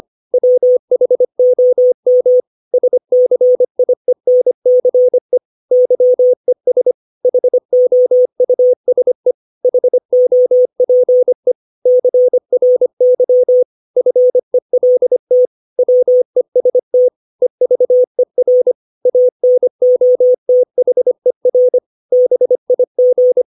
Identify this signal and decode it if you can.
Morse code, heard as a keyed tone, about 25 wpm, 500 Hz. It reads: WHOM SCIENCE YES HOUSE HOPE CRY FELT WEST EVER ANOTHER BIG
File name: 632 words / 632 Words 25WPM